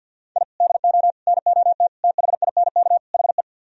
Morse code, heard as a keyed tone, about 64 wpm, 690 Hz. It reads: I DO NOT THINK HE